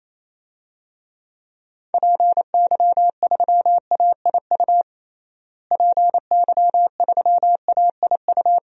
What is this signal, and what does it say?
Morse code, heard as a keyed tone, about 28 wpm, 700 Hz. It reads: PY3AIU PY3AIU
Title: DXCC-100 Call Signs - 6 Characters - Repeated 3x - 28wpm